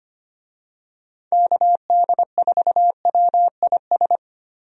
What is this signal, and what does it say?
Morse code, heard as a keyed tone, about 25 wpm, 700 Hz. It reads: KD4WIS
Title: USA Call Signs - 6 Characters - Repeated 2x - 25wpm